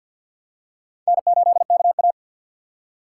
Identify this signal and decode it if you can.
Morse code, heard as a keyed tone, about 51 wpm, 700 Hz. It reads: N8KA
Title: USA Call Signs - 4 Characters - 3x Character Spacing - 50wpm